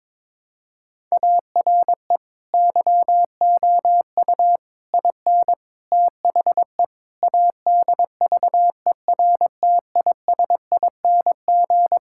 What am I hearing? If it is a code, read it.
Morse code, heard as a keyed tone, about 22 wpm, 700 Hz. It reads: ARE YOU IN THE ADVERTISING
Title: Sentences from Top 1000 Words - Review Part 6 - 22wpm